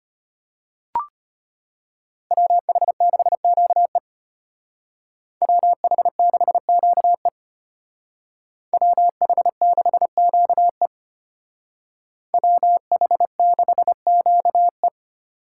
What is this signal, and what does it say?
Morse code, heard as a keyed tone, about 32 wpm, 700 Hz. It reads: WH6QE WH6QE WH6QE WH6QE